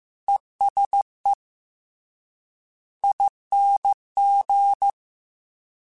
Morse code, heard as a keyed tone, about 15 wpm, 800 Hz. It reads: ESE ING
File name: Short Runs / Short Runs-mp3s / ESE ING @13